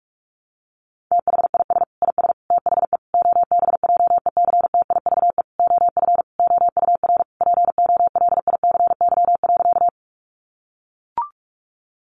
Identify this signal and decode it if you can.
Morse code, heard as a keyed tone, about 44 wpm, 700 Hz. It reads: THIS IS THE OBJECTIVE OF OUR POLICY.
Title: Sentences from Top 1200 Words 45wpm